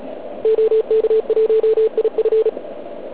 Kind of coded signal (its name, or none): Morse code